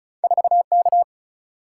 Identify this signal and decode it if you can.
Morse code, heard as a keyed tone, about 36 wpm, 700 Hz. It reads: 4K
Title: Single-Letter Single-Number 35wpm